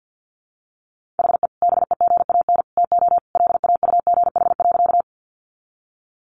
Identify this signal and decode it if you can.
Morse code, heard as a keyed tone, about 50 wpm, 700 Hz. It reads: HE BEGAN TO LAUGH.
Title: Sentences from Top 500 Words 50wpm